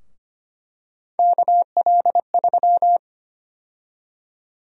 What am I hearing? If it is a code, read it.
Morse code, heard as a keyed tone, about 25 wpm, 700 Hz. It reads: KL3
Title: USA Call Sign Prefixes - 25wpm